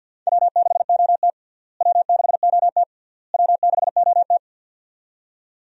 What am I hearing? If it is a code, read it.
Morse code, heard as a keyed tone, about 47 wpm, 700 Hz. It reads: W6OT W6OT W6OT
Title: USA Call Signs - 4 Characters - Repeated 3x - 50wpm